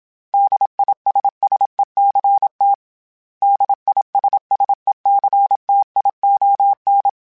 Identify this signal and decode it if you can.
Morse code, heard as a keyed tone, about 27 wpm, 800 Hz. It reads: DISSECT DISSECTION
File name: Special INGs / Special INGs-mp3s / Suffixes @25